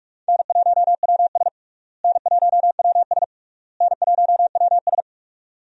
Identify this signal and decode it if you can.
Morse code, heard as a keyed tone, about 46 wpm, 700 Hz. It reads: N1WS N1WS N1WS
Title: USA Call Signs - 4 Characters - Repeated 3x - 45wpm